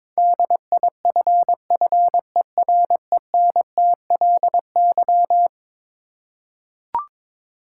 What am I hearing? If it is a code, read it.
Morse code, heard as a keyed tone, about 22 wpm, 700 Hz. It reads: DIFFERENTLY
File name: practice / 22wpm / Eleven Letter Words - 22wpm